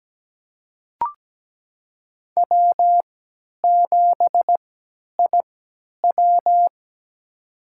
Morse code, heard as a keyed tone, about 17 wpm, spaced as for 10 wpm, 700 Hz. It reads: W7IW
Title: USA Call Signs - 3x Character Spacing - 17wpm